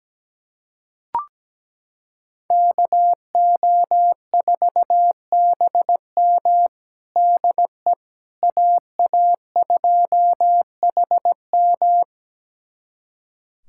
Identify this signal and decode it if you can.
Morse code, heard as a keyed tone, about 17 wpm, 700 Hz. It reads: KO4BM DE AA2HM
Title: USA Callsign DE Callsign - 5 Characters - 17wpm